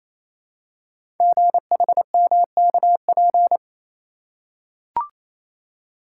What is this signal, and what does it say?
Morse code, heard as a keyed tone, about 28 wpm, 700 Hz. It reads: GHMKP
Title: Code Groups - 5 Letters - 28wpm